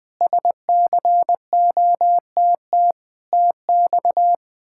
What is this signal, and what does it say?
Morse code, heard as a keyed tone, about 21 wpm, 700 Hz. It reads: SCOTT TX
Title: K1USN SST QSO Practice - 20wpm